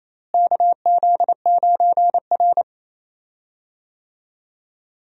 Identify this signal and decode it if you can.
Morse code, heard as a keyed tone, about 28 wpm, 700 Hz. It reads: KZ9R
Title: USA Call Signs - 4 Characters - Repeated 3x - 28wpm